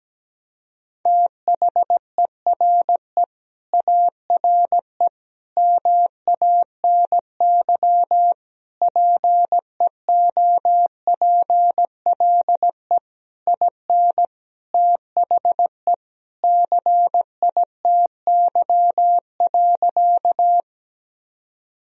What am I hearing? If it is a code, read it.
Morse code, heard as a keyed tone, about 17 wpm, 700 Hz. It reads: THERE ARE MANY PEOPLE IN THE CITY.
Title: Sentences from Top 1000 Words - Review Part 1 - 17wpm